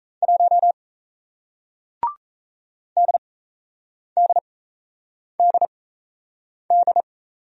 Morse code, heard as a keyed tone, about 37 wpm, spaced as for 8 wpm, 700 Hz. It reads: 1 DDDD